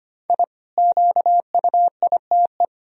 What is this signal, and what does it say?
Morse code, heard as a keyed tone, about 25 wpm, 700 Hz. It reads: I QUITE
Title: Sentences from Top 1000 Words - Review Part 1 - 25wpm